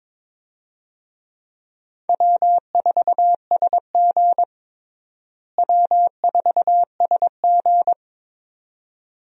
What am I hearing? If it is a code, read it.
Morse code, heard as a keyed tone, about 21 wpm, 700 Hz. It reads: W4SG W4SG
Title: USA Call Signs - 4 Characters - Repeated 2x - 22wpm